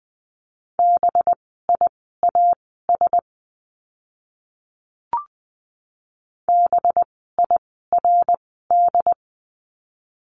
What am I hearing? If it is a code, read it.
Morse code, heard as a keyed tone, about 20 wpm, spaced as for 14 wpm, 700 Hz. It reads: BIAS BIRD
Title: 185 Lesson 30f - Letter b - Words 20wpm